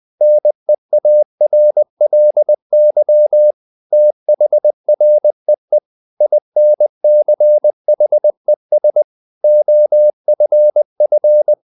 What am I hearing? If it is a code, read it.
Morse code, heard as a keyed tone, about 20 wpm, 600 Hz. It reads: NEARLY THREE INCHES OFF
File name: the old man and the sea part 2 20wpm